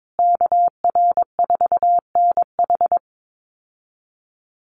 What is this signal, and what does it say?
Morse code, heard as a keyed tone, about 22 wpm, 700 Hz. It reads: KR4NH
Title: USA Call Signs Repeated 3x 22wpm